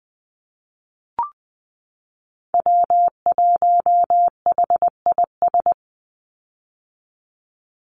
Morse code, heard as a keyed tone, about 20 wpm, 700 Hz. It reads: W1HIS
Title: Call Signs - 20wpm